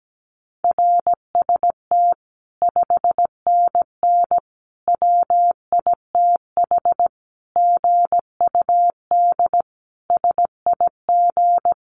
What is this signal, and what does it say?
Morse code, heard as a keyed tone, about 17 wpm, 700 Hz. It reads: RST 5NN WITH GUD SIG
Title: Common QSO Phrases 17wpm